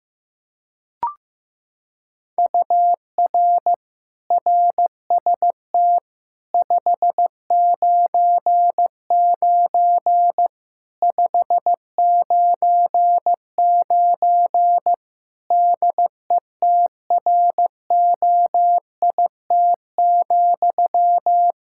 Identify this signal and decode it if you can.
Morse code, heard as a keyed tone, about 15 wpm, 700 Hz. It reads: UR RST 599 599 DETROIT,
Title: Practice QSO - Exchange 1 - Beginning - 15wpm